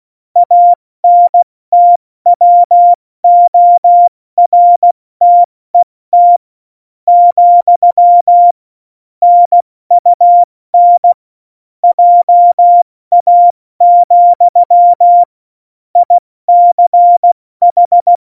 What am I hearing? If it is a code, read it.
Morse code, heard as a keyed tone, about 16 wpm, spaced as for 14 wpm, 700 Hz. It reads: ANTWORTET, NUN JA, ICH
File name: Morsekurs / Witze / Witz 01. Oktober 2019, 70 F80